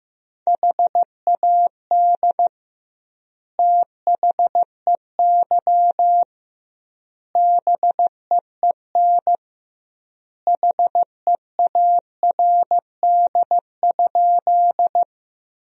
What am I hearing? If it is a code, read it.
Morse code, heard as a keyed tone, about 15 wpm, 700 Hz. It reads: HAD THEY BEEN HEARD?
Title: 2x Word-Spacing - Sentences from Top 400 Words - 15wpm